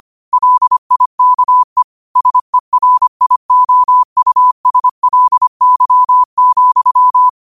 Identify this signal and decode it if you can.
Morse code, heard as a keyed tone, about 25 wpm, 1000 Hz. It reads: LIKE SERIOUSLY,